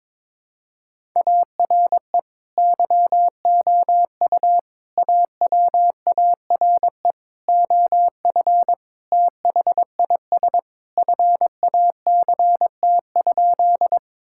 Morse code, heard as a keyed tone, about 22 wpm, 700 Hz. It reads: ARE YOU AWARE OF THIS FACT?
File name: practice / 22wpm / Sentences with 2-Word Phrases - 22wpm